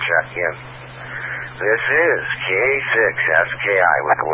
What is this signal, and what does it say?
Emergency Alert System (SAME) tones, heard as harsh buzzing bursts, with speech. It is heard from 1.1 to 3.6 s.